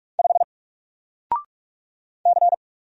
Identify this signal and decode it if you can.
Morse code, heard as a keyed tone, about 44 wpm, 700 Hz. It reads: 5 C
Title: Single Letter-Number Alt 45wpm